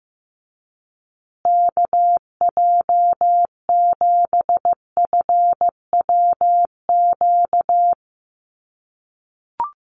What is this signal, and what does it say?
Morse code, heard as a keyed tone, about 15 wpm, 700 Hz. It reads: KJ7FWQ
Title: USA Call Signs Repeated 2x 15wpm